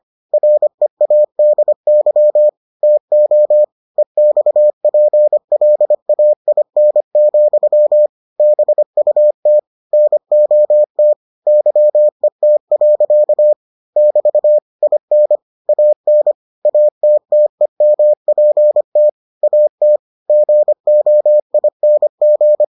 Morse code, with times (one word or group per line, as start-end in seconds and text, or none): 0.3-2.5 READY
2.8-3.6 TO
4.0-8.1 EXPLAIN,
8.4-9.6 BUT
9.9-11.1 NOT
11.5-13.5 YET.
14.0-14.6 <BT>
14.8-15.4 IN
15.7-16.3 AN
16.7-19.1 ATTEMPT
19.4-20.0 AT
20.3-22.6 GOING